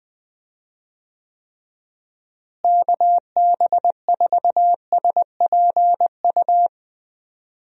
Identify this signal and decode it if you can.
Morse code, heard as a keyed tone, about 21 wpm, 700 Hz. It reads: KB4SPU
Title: USA Call Signs - 6 Characters - Repeated 3x - 20wpm